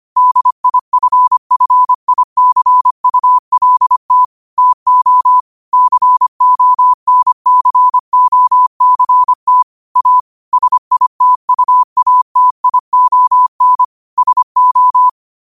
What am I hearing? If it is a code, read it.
Morse code, heard as a keyed tone, about 25 wpm, 1000 Hz. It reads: DIFFICULT TO CONCOCT A SITUATION SO
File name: qotdfiles / 25wpm